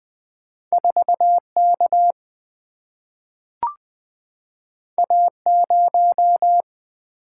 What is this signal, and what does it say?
Morse code, heard as a keyed tone, about 20 wpm, 700 Hz. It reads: 4K A0
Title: Single-Letter Single-Number 20wpm